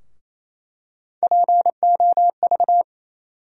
Morse code, heard as a keyed tone, about 28 wpm, 700 Hz. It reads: POV
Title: USA Call Sign Postfixes - 3 Letters 28wpm